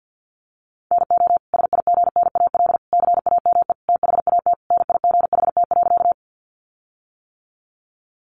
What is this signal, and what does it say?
Morse code, heard as a keyed tone, about 50 wpm, 700 Hz. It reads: NO SIGNAL CAME THAT NIGHT.
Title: Sentences from Top 1400 Words 50wpm